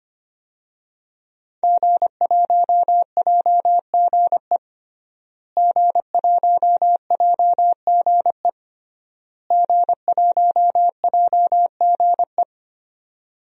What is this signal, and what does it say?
Morse code, heard as a keyed tone, about 25 wpm, 700 Hz. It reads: G1JGE G1JGE G1JGE
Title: DXCC-100 Call Signs - 5 Characters - Repeated 3x - 25wpm